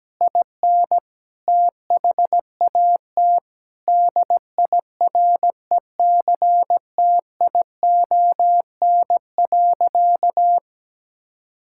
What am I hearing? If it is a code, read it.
Morse code, heard as a keyed tone, about 17 wpm, 700 Hz. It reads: IN THAT DIRECTION.
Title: Sentences from Top 1000 Words - Review Part 6 - 17wpm